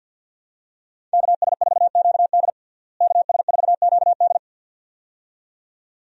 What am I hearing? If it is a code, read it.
Morse code, heard as a keyed tone, about 52 wpm, 700 Hz. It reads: KS4QD KS4QD
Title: USA Call Signs - 5 Characters - Repeated 2x - 50wpm